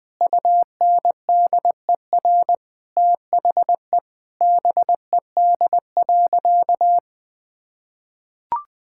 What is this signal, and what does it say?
Morse code, heard as a keyed tone, about 20 wpm, 700 Hz. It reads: UNDER THE BED.
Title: Sentences with 2-Word Phrases - Spoken Beginning - 20wpm